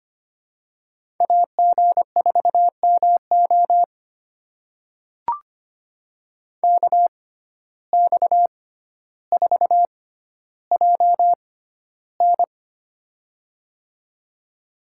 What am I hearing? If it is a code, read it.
Morse code, heard as a keyed tone, about 25 wpm, spaced as for 13 wpm, 700 Hz. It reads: AG4MO KX4JN
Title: USA Call Signs - 5 Characters - 6x Character Spacing - 25wpm